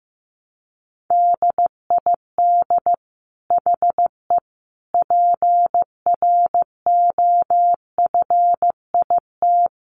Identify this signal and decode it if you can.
Morse code, heard as a keyed tone, about 15 wpm, 700 Hz. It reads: DID HE PROFIT
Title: Sentences from Top 800 Words 15wpm